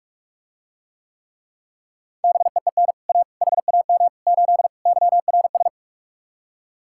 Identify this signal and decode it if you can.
Morse code, heard as a keyed tone, about 48 wpm, 700 Hz. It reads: BEEN A HAM 8 YRS